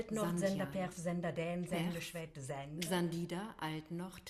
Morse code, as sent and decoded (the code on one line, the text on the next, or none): none